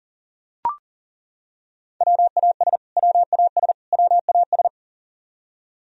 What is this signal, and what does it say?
Morse code, heard as a keyed tone, about 40 wpm, 700 Hz. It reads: WAS WAS WAS
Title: Top 100 Words Repeated 40wpm